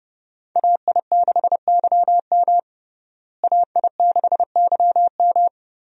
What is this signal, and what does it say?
Morse code, heard as a keyed tone, about 30 wpm, 700 Hz. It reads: AI6YM AI6YM
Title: USA Call Signs - 5 Characters - Repeated 3x - 30wpm